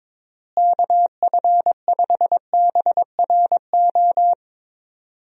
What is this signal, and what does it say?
Morse code, heard as a keyed tone, about 22 wpm, 700 Hz. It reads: KF5BRO
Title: USA Call Signs - 6 Characters - Repeated 3x - 22wpm